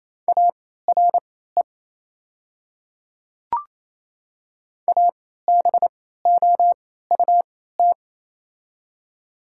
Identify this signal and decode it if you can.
Morse code, heard as a keyed tone, about 28 wpm, spaced as for 16 wpm, 700 Hz. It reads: ARE ABOUT
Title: 3x Character Spacing - Top 200 Words - 28wpm